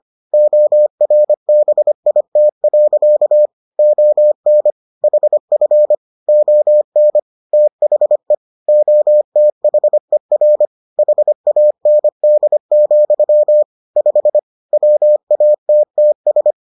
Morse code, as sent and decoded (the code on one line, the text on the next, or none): --- .-. -... .. - .-.-.- / --- -. / .... ..-. / --- -. / - .... / --- - .... .-. / .... .- -. -.. --..-- / ..... / .-- .- - - ...
ORBIT. ON HF ON THE OTHER HAND, 5 WATTS